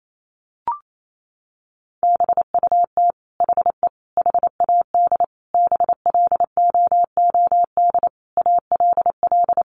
Morse code, heard as a keyed tone, about 28 wpm, 700 Hz. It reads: BUT HE HAD BLOOD ALL
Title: Sentences from Top 1100 Words 28wpm